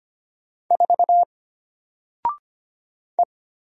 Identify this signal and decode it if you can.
Morse code, heard as a keyed tone, about 25 wpm, 700 Hz. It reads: 4 E
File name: Single-Letter-Number-25wpm